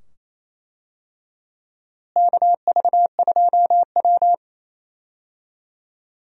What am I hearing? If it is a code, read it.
Morse code, heard as a keyed tone, about 28 wpm, 700 Hz. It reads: KV2W
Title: USA Call Signs - 4 Characters - Repeated 2x - 28wpm